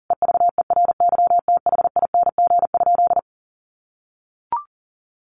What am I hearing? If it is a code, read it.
Morse code, heard as a keyed tone, about 40 wpm, 700 Hz. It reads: EVERYTHING?
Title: Sentences from Top 1100 Words 40wpm